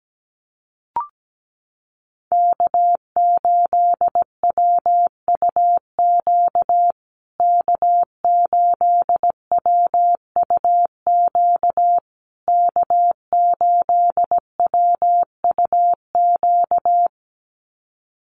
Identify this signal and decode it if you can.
Morse code, heard as a keyed tone, about 17 wpm, 700 Hz. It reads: K8WUQ K8WUQ K8WUQ
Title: USA Call Signs Repeated 3x 17wpm